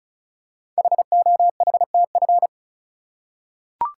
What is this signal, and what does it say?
Morse code, heard as a keyed tone, about 36 wpm, 700 Hz. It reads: HOHTF
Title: Code Groups - 5 Letters - 35wpm